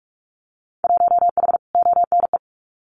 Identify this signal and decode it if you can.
Morse code, heard as a keyed tone, about 44 wpm, 700 Hz. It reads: 1H ONE
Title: Winter Field Day Exchanges 45wpm